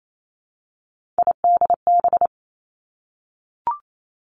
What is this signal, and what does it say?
Morse code, heard as a keyed tone, about 28 wpm, 700 Hz. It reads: IDB
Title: Three Challenging Letter-Number Combinations 28wpm